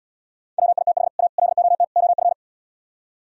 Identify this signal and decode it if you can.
Morse code, heard as a keyed tone, about 77 wpm, 700 Hz. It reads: WITH A LOT OF